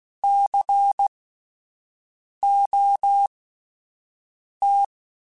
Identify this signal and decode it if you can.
Morse code, heard as a keyed tone, about 16 wpm, spaced as for 5 wpm, 800 Hz. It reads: COT